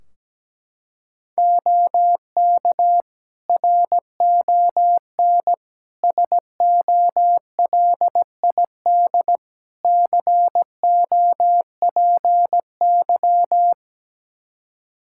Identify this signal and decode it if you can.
Morse code, heard as a keyed tone, about 17 wpm, 700 Hz. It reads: OK RON SOLID COPY
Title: Practice QSO - Exchange 3 - More Info - 17wpm